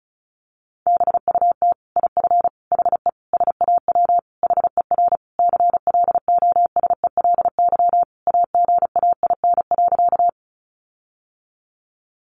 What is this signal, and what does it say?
Morse code, heard as a keyed tone, about 34 wpm, 700 Hz. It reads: BUT IF HE SAW HER CLOSELY AGAIN.
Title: Sentences from Top 1400 Words 35wpm